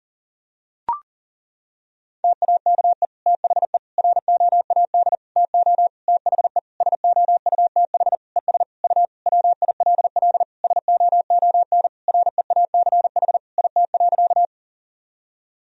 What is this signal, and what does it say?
Morse code, heard as a keyed tone, about 40 wpm, 700 Hz. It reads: TAKE THE ROAD TO THE SOUTH ES U WILL SOON REACH IT.
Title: Sentences from Top 400 Words and Abbreviations 40wpm